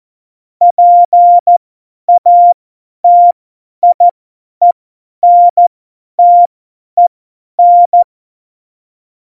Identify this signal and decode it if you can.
Morse code, heard as a keyed tone, about 14 wpm, spaced as for 10 wpm, 700 Hz. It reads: PATIENTEN
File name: Morsekurs / Witze / Witz 13 March 2018, 50 F70